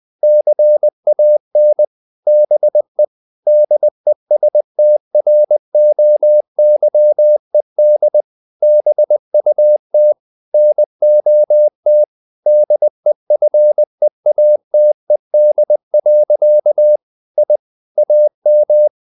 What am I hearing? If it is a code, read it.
Morse code, heard as a keyed tone, about 20 wpm, 600 Hz. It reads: CAN BE DESTROYED BUT NOT DEFEATED. I AM
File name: the old man and the sea part 2 20wpm